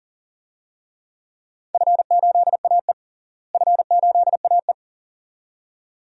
Morse code, heard as a keyed tone, about 40 wpm, 700 Hz. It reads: F8AE F8AE
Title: DXCC-100 Call Signs - 4 Characters - Repeated 2x - 40wpm